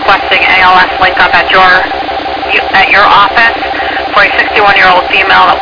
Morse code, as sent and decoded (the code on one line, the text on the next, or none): none